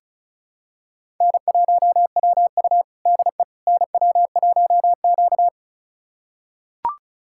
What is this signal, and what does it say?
Morse code, heard as a keyed tone, about 34 wpm, 700 Hz. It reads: N1WU DE NW1Q
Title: USA Callsign DE Callsign - 4 Characters - 35wpm